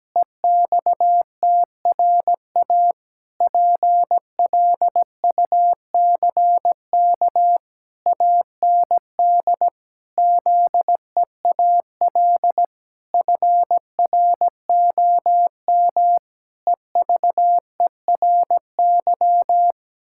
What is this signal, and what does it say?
Morse code, heard as a keyed tone, about 17 wpm, 700 Hz. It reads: EXTRA PLUCK AND ZEAL FROM EVERY
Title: Pangram Challenge v2 - 17wpm